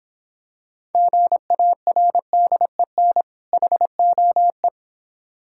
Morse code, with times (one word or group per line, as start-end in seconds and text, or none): 0.9-3.2 GARDEN
3.5-4.7 HOE